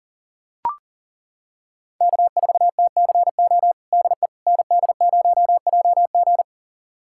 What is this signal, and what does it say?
Morse code, heard as a keyed tone, about 40 wpm, 700 Hz. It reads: K4TCO DE ND0JG
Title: USA Callsign DE Callsign - 5 Characters - 40wpm